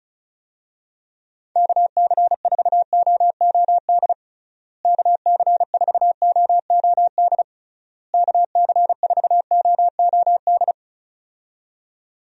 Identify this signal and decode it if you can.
Morse code, heard as a keyed tone, about 36 wpm, 700 Hz. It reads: KC4OOD KC4OOD KC4OOD
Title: USA Call Signs - 6 Characters - Repeated 3x - 35wpm